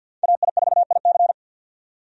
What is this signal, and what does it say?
Morse code, heard as a keyed tone, about 50 wpm, 700 Hz. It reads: AI4IC
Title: USA Call Signs - 5 Characters - 50wpm